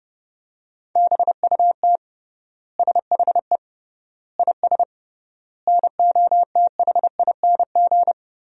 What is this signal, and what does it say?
Morse code, heard as a keyed tone, about 30 wpm, 700 Hz. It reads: BUT SHE IS NOTHING
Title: 3x Word-Spacing - Sentences from Top 500 Words - 30wpm